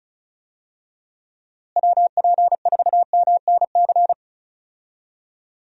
Morse code, heard as a keyed tone, about 34 wpm, 700 Hz. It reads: WP4MNC
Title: USA Call Signs - 6 Characters - Repeated 2x - 35wpm